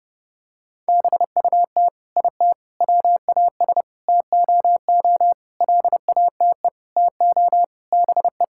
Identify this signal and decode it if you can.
Morse code, heard as a keyed tone, about 32 wpm, 700 Hz. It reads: BUT IT WAS TOO LATE TO BE
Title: Sentences from Top 1600 Words 30wpm